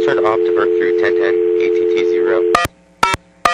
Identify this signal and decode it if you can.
Dial tone (North America), 350 + 440 Hz, with speech.
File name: TurtleCreekEWSD